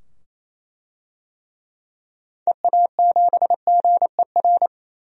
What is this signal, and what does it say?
Morse code, heard as a keyed tone, about 27 wpm, 700 Hz. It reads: EA7GER
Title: DXCC-100 Call Signs - 6 Characters - 28wpm